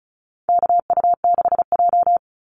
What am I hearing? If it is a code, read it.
Morse code, heard as a keyed tone, about 34 wpm, 700 Hz. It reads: KU6J
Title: NA SOTA Activator Callsigns 35wpm